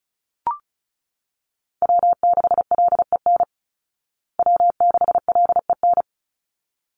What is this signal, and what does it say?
Morse code, heard as a keyed tone, about 34 wpm, 700 Hz. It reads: W6LEN W6LEN
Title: Call Signs Repeated 3x - 35wpm